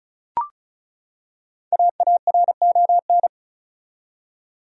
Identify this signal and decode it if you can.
Morse code, heard as a keyed tone, about 36 wpm, 700 Hz. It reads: AARON
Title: USA Common Names - 35wpm